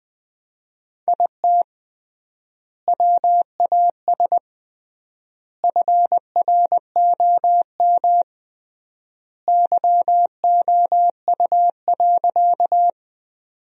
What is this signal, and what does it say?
Morse code, heard as a keyed tone, about 20 wpm, 700 Hz. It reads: IT WAS FROM YOU.
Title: 3x Word-Spacing - Sentences from Top 100 Words - 20wpm